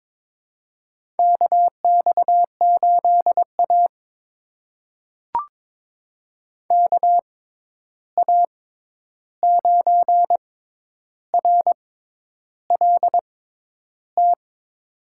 Morse code, heard as a keyed tone, about 22 wpm, spaced as for 10 wpm, 700 Hz. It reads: KX8A KA9RLT
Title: USA Call Signs - 6x Character Spacing - 22wpm